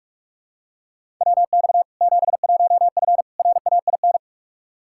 Morse code, heard as a keyed tone, about 46 wpm, 700 Hz. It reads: WX 71F RAIN
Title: K1USN SST QSO Practice 45wpm